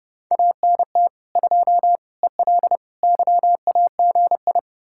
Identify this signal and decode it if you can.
Morse code, heard as a keyed tone, about 40 wpm, spaced as for 35 wpm, 700 Hz. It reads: ANT 2 EL YAGI
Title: Speed-Racing Common QSO Phrases Encore - 30wpm